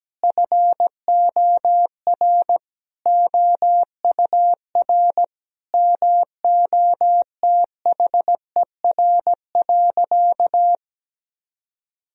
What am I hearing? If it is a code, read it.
Morse code, heard as a keyed tone, about 17 wpm, 700 Hz. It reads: FOR OUR MOTHER.
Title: Sentences from Top 1000 Words - Review Part 2 - 17wpm